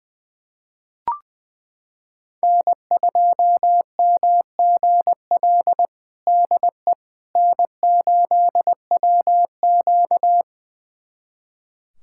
Morse code, heard as a keyed tone, about 20 wpm, 700 Hz. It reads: N2MGL DE N8WQ
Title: USA Callsign DE Callsign 20wpm